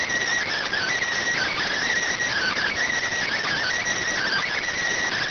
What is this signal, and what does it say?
Emergency Alert System (SAME) tones, heard as harsh buzzing bursts. It is heard from 0.6 to 4.6 s.